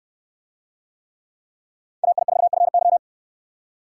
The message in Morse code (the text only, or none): AI4FY